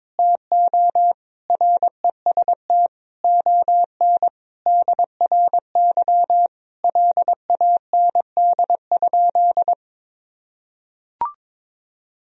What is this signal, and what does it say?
Morse code, heard as a keyed tone, about 22 wpm, 700 Hz. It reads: TO REST ON DRY LAND?
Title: Sentences from Top 1000 Words - Review Part 3 - 22wpm